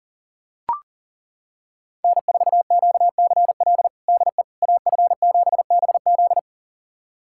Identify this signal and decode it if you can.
Morse code, heard as a keyed tone, about 39 wpm, 700 Hz. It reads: N4QCL DE AF7BZ
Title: USA Callsign DE Callsign - 5 Characters - 40wpm